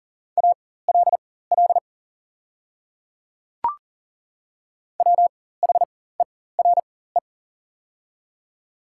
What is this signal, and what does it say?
Morse code, heard as a keyed tone, about 40 wpm, spaced as for 19 wpm, 700 Hz. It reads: ALL WHERE
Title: 4x Character Spacing - Top 300 Words - 40wpm